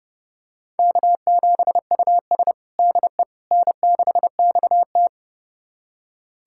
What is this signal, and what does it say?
Morse code, heard as a keyed tone, about 30 wpm, 700 Hz. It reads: K7US DE N6XT
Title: USA Callsign DE Callsign - 4 Characters - 30wpm